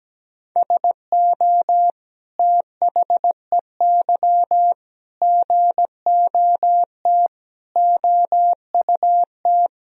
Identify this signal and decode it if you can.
Morse code, heard as a keyed tone, about 17 wpm, 700 Hz. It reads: SO THEY GOT OUT
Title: Sentences from Top 300 Words and Abbreviations 17wpm